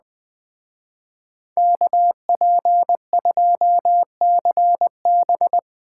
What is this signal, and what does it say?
Morse code, heard as a keyed tone, about 20 wpm, 700 Hz. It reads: KP2CB